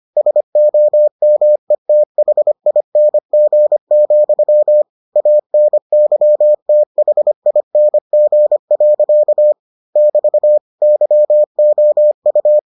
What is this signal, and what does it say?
Morse code, heard as a keyed tone, about 25 wpm, 600 Hz. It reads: SOMETHING, ANYTHING. <BT> YOU